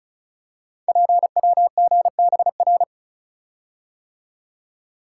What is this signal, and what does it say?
Morse code, heard as a keyed tone, about 36 wpm, 700 Hz. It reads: PWGBR
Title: Code Groups - 5 Letters - 35wpm